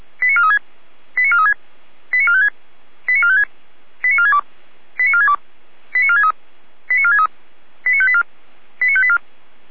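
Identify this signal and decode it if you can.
Five-tone selective call (ZVEI), heard as quick runs of five tones, tones of about 70 ms. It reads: 89436 89436 89456 89456 89462 89462 89463 89463 89584 89584
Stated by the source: Five-tone ZVEI selective call 89417, received on an SDR.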